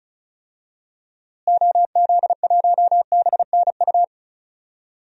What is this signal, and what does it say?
Morse code, heard as a keyed tone, about 36 wpm, 700 Hz. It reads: OZ1BNU
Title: DXCC-100 Call Signs - 6 Characters - Repeated 2x - 35wpm